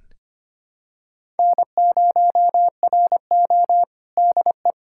Morse code, heard as a keyed tone, about 25 wpm, 700 Hz. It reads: N0RO DE
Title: USA Callsign DE Callsign 25wpm